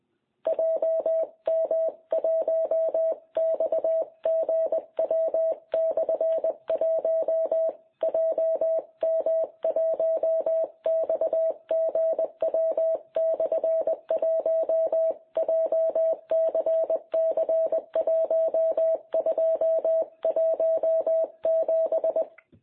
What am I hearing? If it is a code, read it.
Morse code, heard as a keyed tone, about 20 wpm, 660 Hz. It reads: JM1XGW/1 JM1XGW/1 JCC1217
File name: jh8cbh / 2021cw